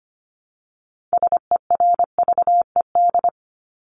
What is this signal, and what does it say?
Morse code, heard as a keyed tone, about 25 wpm, 700 Hz. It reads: SERVED
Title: ED Suffix - Top 100 Words with Base Words 25wpm